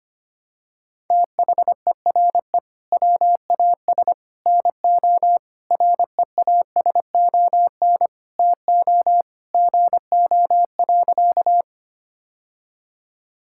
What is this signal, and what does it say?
Morse code, heard as a keyed tone, about 25 wpm, 700 Hz. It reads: THERE WAS NO REASON TO GO.
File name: practice / 25wpm / Sentences with 3-Word Phrases - 25wpm